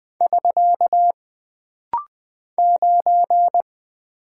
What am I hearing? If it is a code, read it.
Morse code, heard as a keyed tone, about 20 wpm, 700 Hz. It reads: <SK> 9
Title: Single Character (Letter:Number:Punctuation:Prosign) - Rapid-Fire - 20wpm